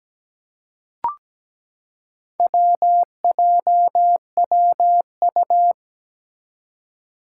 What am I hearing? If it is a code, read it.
Morse code, heard as a keyed tone, about 17 wpm, 700 Hz. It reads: WJWU
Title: Four Challenging Letter-Number Combinations 17wpm